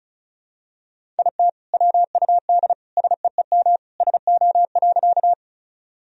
Morse code, heard as a keyed tone, about 36 wpm, 700 Hz. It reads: IT WUD SEEM SO.
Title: Sentences from Top 300 Words and Abbreviations 35wpm